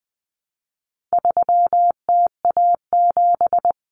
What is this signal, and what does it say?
Morse code, heard as a keyed tone, about 20 wpm, 700 Hz. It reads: 3TA7
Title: Common Cut Numbers - 4-Digit 20wpm